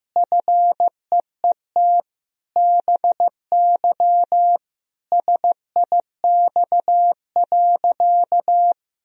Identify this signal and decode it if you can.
Morse code, heard as a keyed tone, about 15 wpm, 700 Hz. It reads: FEET BY SIX.
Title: Sentences from Top 1000 Words - Review Part 2 - 15wpm